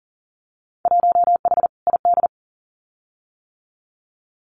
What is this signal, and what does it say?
Morse code, heard as a keyed tone, about 40 wpm, 700 Hz. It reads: 1H ID
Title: Winter Field Day Exchanges 40wpm